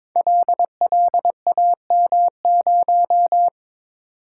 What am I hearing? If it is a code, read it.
Morse code, heard as a keyed tone, about 22 wpm, 700 Hz. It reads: LLAM0
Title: Code Groups - 5 Letter:Numbers - 22wpm